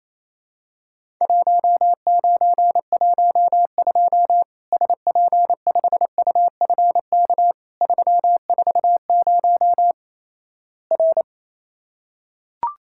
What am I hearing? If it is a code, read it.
Morse code, heard as a keyed tone, about 27 wpm, 700 Hz. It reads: 1912 SP5UFK 340 R
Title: Worked All Europe DX Contest - QTC Exchange - 28wpm